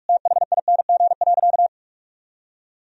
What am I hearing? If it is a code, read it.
Morse code, heard as a keyed tone, about 46 wpm, 700 Hz. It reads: THING.
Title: Sentences from Top 1000 Words - Review Part 3 - 45wpm